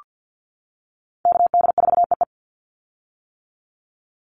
Morse code, heard as a keyed tone, about 48 wpm, 700 Hz. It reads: KD4EE
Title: Call Signs - 50wpm